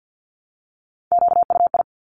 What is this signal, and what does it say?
Morse code, heard as a keyed tone, about 50 wpm, 700 Hz. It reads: QUI
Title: Top Three Letter Combinations in Words 50wpm